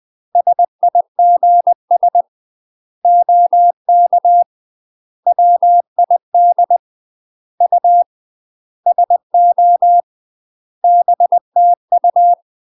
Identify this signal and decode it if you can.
Morse code, heard as a keyed tone, about 20 wpm, 700 Hz. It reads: SIGS OK WID U SO BTU
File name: UK MASTER QSOs TRAINING / 20WPM / UK MASTER QSOs 0026